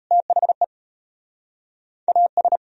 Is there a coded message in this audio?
Yes: Morse code from 0.1 to 2.6 s, heard as a keyed tone, about 41 wpm, 700 Hz. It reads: THE AS